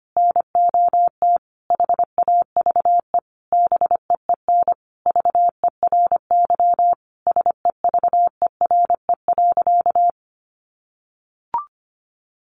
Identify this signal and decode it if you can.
Morse code, heard as a keyed tone, about 25 wpm, 700 Hz. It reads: NOT HAVE BEEN VERY SEVERE.
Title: Sentences from Top 1400 Words 25wpm